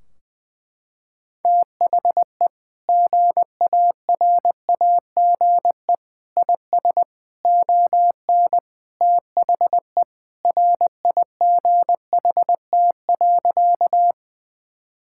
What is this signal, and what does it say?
Morse code, heard as a keyed tone, about 20 wpm, 700 Hz. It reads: THE GARAGE IS ON THE RIGHT.
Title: Sentences from Top 1600 Words 20wpm